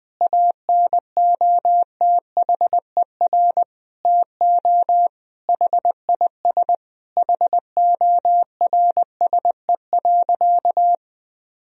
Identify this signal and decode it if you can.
Morse code, heard as a keyed tone, about 20 wpm, 700 Hz. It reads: ANOTHER TO HIS HORSE.
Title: Sentences from Top 1000 Words - Review Part 1 - 20wpm